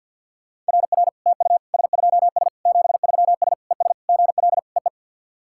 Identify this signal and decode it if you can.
Morse code, heard as a keyed tone, about 50 wpm, 690 Hz. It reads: RR TU S2S 73S ES GL EE
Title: SOTA QSO Practice 50wpm